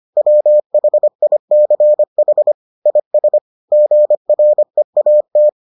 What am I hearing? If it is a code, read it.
Morse code, heard as a keyed tone, about 25 wpm, 600 Hz. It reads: WHICH IS GREAT